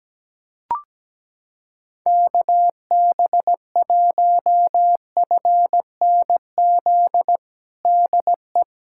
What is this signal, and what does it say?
Morse code, heard as a keyed tone, about 17 wpm, 700 Hz. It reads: KB1FNZ DE
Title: USA Callsign DE Callsign - 6 Characters - 17wpm